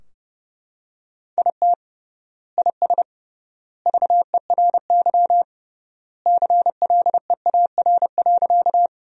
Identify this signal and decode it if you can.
Morse code, heard as a keyed tone, about 30 wpm, 700 Hz. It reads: IT IS VERY CLEAR.
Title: 3x Word-Spacing - Sentences from Top 500 Words - 30wpm